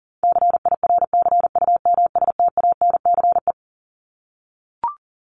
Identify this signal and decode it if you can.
Morse code, heard as a keyed tone, about 40 wpm, 700 Hz. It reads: CIRCUMSTANCE
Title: ES:S Suffix - Top 100 Words with Base Words 40wpm